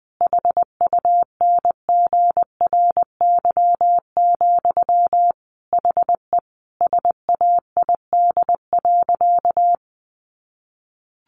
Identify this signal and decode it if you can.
Morse code, heard as a keyed tone, about 20 wpm, 700 Hz. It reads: HUNGRY, HE SAID.
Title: Sentences from Top 1400 Words 20wpm